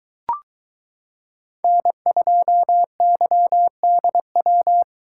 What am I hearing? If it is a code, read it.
Morse code, heard as a keyed tone, about 22 wpm, 700 Hz. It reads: N2YDW